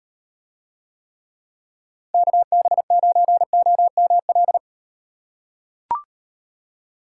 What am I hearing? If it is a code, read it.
Morse code, heard as a keyed tone, about 38 wpm, 700 Hz. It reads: KB9OML